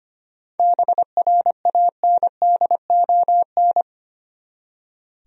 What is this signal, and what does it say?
Morse code, heard as a keyed tone, about 25 wpm, 700 Hz. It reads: BRANDON
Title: USA Common Names - 25wpm